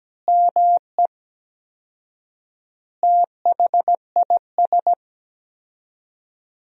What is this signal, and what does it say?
Morse code, heard as a keyed tone, about 17 wpm, 700 Hz. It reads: ME THIS
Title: 4x Word-Spacing - Sentences from Top 400 Words - 17wpm